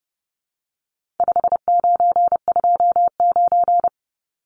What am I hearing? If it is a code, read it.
Morse code, heard as a keyed tone, about 30 wpm, 700 Hz. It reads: 5929
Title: Numbers - 4-Digit 30wpm